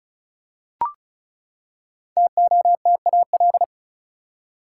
Morse code, heard as a keyed tone, about 36 wpm, 700 Hz. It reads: TOTAL
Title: Five Letter Words - 35wpm